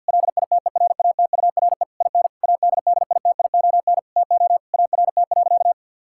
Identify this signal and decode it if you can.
Morse code, heard as a keyed tone, about 50 wpm, 700 Hz. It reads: LITERATURE IN ADDITION TO ART.
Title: Sentences from Top 1000 Words - Review Part 6 - 50wpm